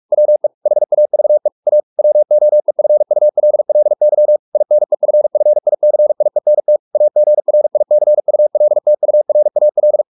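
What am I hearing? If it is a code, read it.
Morse code, heard as a keyed tone, about 44 wpm, 600 Hz. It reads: WE HAVE A WOEFULLY INEFFICIENT AGRICULTURAL
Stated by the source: Morse practice at 45 wpm, the novel Never by Ken Follett in English, from CW med Gnister.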